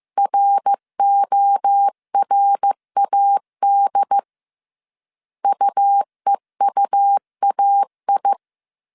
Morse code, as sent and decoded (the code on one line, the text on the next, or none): .-. --- .-. .- -.. / ..- . ..- .- ..
RORAD UEUAI